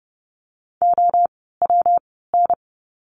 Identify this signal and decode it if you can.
Morse code, heard as a keyed tone, about 30 wpm, spaced as for 17 wpm, 700 Hz. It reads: OWN
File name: course / 30:10wpm / 237 Lesson 38f - Letter q - Words 30wpm